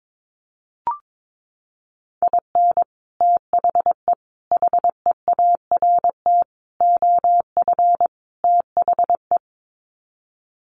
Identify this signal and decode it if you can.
Morse code, heard as a keyed tone, about 22 wpm, 700 Hz. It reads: IN THE HEART OF THE
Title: Sets of 5 Words 22wpm